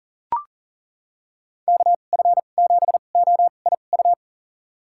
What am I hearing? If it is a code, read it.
Morse code, heard as a keyed tone, about 40 wpm, spaced as for 29 wpm, 700 Hz. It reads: KF7OIU
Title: USA Call Signs - 6 Characters - 2x Character Spacing - 40wpm